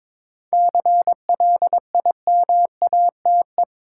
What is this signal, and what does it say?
Morse code, heard as a keyed tone, about 22 wpm, 700 Hz. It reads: CLIMATE
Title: Sentences from Top 1600 Words - New Word Drill Alt - 22wpm